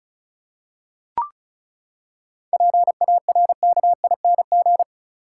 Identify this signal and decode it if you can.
Morse code, heard as a keyed tone, about 36 wpm, 700 Hz. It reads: PARKING
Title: Seven Letter Words - 35wpm